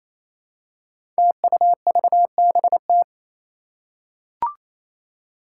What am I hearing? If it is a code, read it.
Morse code, heard as a keyed tone, about 29 wpm, 700 Hz. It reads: TUVBT
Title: Five Challenging Letter-Number Combinations 28wpm